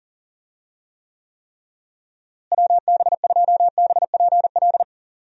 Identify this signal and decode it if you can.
Morse code, heard as a keyed tone, about 38 wpm, 700 Hz. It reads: WB2BPL
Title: USA Call Signs - 6 Characters - Repeated 3x - 40wpm